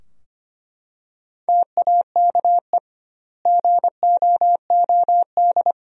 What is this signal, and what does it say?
Morse code, heard as a keyed tone, about 25 wpm, 700 Hz. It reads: TAKE GOOD
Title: 2x Word-Spacing - Sentences from Top 400 Words - 25wpm